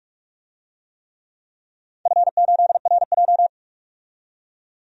Morse code, heard as a keyed tone, about 44 wpm, 700 Hz. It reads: F8RJ
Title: DXCC-100 Call Signs - 4 Characters - 45wpm